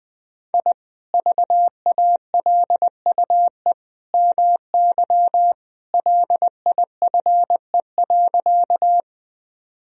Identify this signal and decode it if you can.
Morse code, heard as a keyed tone, about 20 wpm, 700 Hz. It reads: I VALUE MY LIFE.
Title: Sentences from Top 1000 Words - Review Part 3 - 20wpm